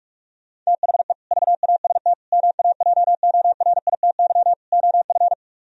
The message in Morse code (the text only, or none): THE VAST MAJORITY OF